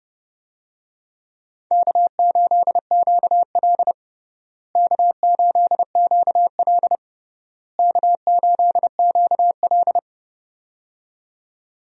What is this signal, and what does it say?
Morse code, heard as a keyed tone, about 30 wpm, 700 Hz. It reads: K8QL K8QL K8QL
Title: USA Call Signs - 4 Characters - Repeated 3x - 30wpm